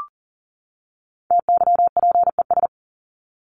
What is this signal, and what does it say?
Morse code, heard as a keyed tone, about 40 wpm, 700 Hz. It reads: TYPES
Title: ES:S Suffix - Top 100 Words with Base Words 40wpm